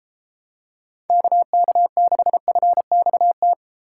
Morse code, heard as a keyed tone, about 33 wpm, 700 Hz. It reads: KK6FXT